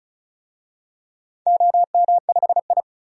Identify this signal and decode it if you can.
Morse code, heard as a keyed tone, about 36 wpm, 700 Hz. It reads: OM5I
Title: DXCC-100 Call Signs - 4 Characters - 35wpm